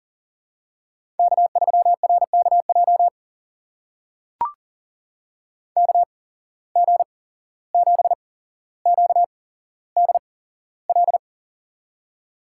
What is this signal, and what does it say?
Morse code, heard as a keyed tone, about 40 wpm, spaced as for 16 wpm, 700 Hz. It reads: K3RKJ KG7QDL
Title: USA Call Signs - 8x Character Spacing - 40wpm